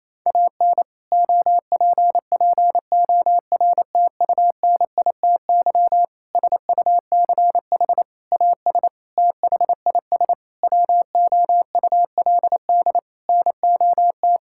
Morse code, heard as a keyed tone, about 28 wpm, 700 Hz. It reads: AN OPPORTUNITY SUCH AS THIS WOULD NOT
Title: Sentences from Top 1600 Words 28wpm